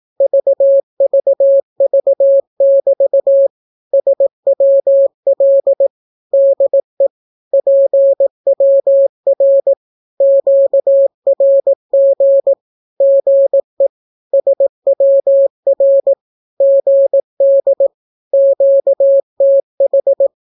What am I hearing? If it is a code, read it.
Morse code, heard as a keyed tone, about 18 wpm, 550 Hz. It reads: VVV<BT> SWL DE PWR QRG GE SWR GD QTH
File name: naukapodstaw / skroty / skroty